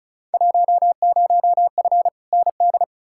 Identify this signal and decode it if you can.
Morse code, heard as a keyed tone, about 36 wpm, 690 Hz. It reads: 10F ND
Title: Field Day Exchanges Encore - 35wpm